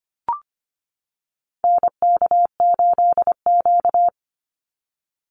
Morse code, heard as a keyed tone, about 25 wpm, 700 Hz. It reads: NK8Q
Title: NA SOTA Activator Callsigns 25wpm